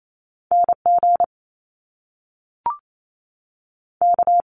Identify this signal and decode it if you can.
Morse code, heard as a keyed tone, about 28 wpm, 700 Hz. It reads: N G K
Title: Top Two Letter Combinations in Words 28wpm